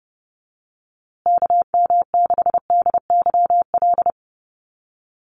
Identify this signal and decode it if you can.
Morse code, heard as a keyed tone, about 30 wpm, 700 Hz. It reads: KM6DYL
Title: USA Call Signs Repeated 3x 30wpm